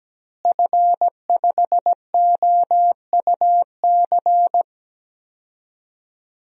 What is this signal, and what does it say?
Morse code, heard as a keyed tone, about 17 wpm, 700 Hz. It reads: F5OUC
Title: DXCC-100 Call Signs - 5 Characters - Repeated 2x - 17wpm